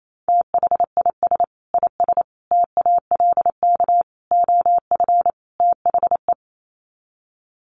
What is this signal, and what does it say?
Morse code, heard as a keyed tone, about 28 wpm, 700 Hz. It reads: THIS IS TALK OF THE
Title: Sets of 5 Words 28wpm